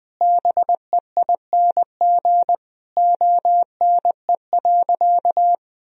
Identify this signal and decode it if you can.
Morse code, heard as a keyed tone, about 20 wpm, 700 Hz. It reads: BEING ONE.
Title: Sentences from Top 1000 Words - Review Part 3 - 20wpm